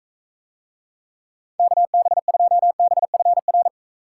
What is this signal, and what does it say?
Morse code, heard as a keyed tone, about 43 wpm, 700 Hz. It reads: KB2BFR